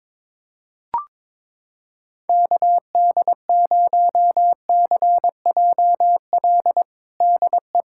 Morse code, heard as a keyed tone, about 22 wpm, 700 Hz. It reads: KD0CJL DE
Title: USA Callsign DE Callsign 22wpm